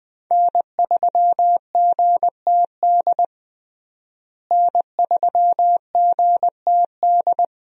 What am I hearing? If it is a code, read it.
Morse code, heard as a keyed tone, about 20 wpm, 700 Hz. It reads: N3GTD N3GTD
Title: USA Call Signs - 5 Characters - Repeated 2x - 20wpm